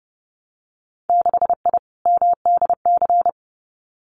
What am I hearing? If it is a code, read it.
Morse code, heard as a keyed tone, about 30 wpm, 700 Hz. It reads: 6I MDC
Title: Winter Field Day Exchanges Encore 30wpm